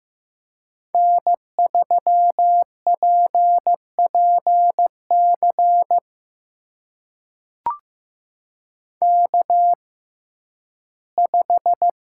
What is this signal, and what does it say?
Morse code, heard as a keyed tone, about 15 wpm, spaced as for 11 wpm, 700 Hz. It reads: N3PPC K5
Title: USA Call Signs - 5 Characters - 6x Character Spacing - 15wpm